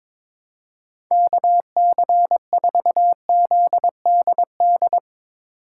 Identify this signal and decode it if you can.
Morse code, heard as a keyed tone, about 22 wpm, 700 Hz. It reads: KC4ZDD
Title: USA Call Signs - 2x Character Spacing - 22wpm